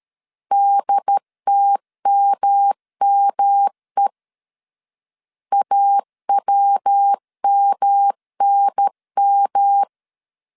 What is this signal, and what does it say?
Morse code, heard as a keyed tone, about 13 wpm, 800 Hz. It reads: DTMME AWMNM
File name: CW / DET / 1 3 1500C S C 12wpm 2750